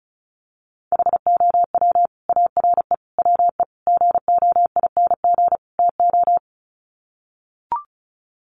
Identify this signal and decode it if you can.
Morse code, heard as a keyed tone, about 34 wpm, 700 Hz. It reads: HOW ARE WE GOING TO
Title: Sets of 5 Words Encore 35wpm